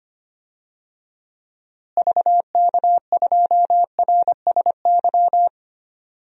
Morse code, heard as a keyed tone, about 25 wpm, 700 Hz. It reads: VK2RSY
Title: DXCC-100 Call Signs - 6 Characters - Repeated 2x - 25wpm